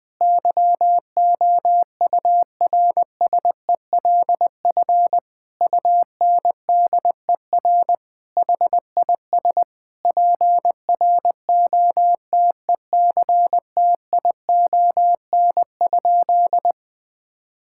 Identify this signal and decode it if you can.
Morse code, heard as a keyed tone, about 20 wpm, 700 Hz. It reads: YOURSELF UNDER HIS PROTECTION?
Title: Sentences from Top 1600 Words 20wpm